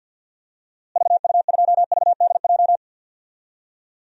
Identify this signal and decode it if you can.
Morse code, heard as a keyed tone, about 49 wpm, 700 Hz. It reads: VU2VDJ
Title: DXCC-100 Call Signs - 6 Characters - Repeated 2x - 50wpm